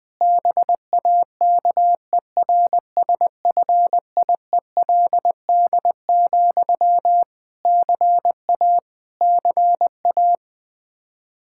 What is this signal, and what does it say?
Morse code, heard as a keyed tone, about 21 wpm, 700 Hz. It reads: BAKERSFIELD, CA CA
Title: Practice QSO - Exchange 1 - Beginning - 20wpm